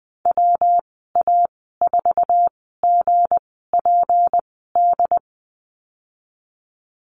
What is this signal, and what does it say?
Morse code, heard as a keyed tone, about 20 wpm, spaced as for 15 wpm, 700 Hz. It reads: WA4GPD
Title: 186 Lesson 30g - Letter b - Call Signs 20wpm